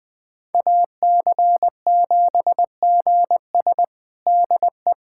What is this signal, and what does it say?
Morse code, heard as a keyed tone, about 20 wpm, 700 Hz. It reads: AC7GS DE
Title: USA Callsign DE Callsign - 5 Characters - 20wpm